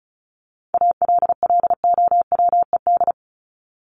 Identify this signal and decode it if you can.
Morse code, heard as a keyed tone, about 34 wpm, 700 Hz. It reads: ALLOWED
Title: ED Suffix - Top 100 Words with Base Words 35wpm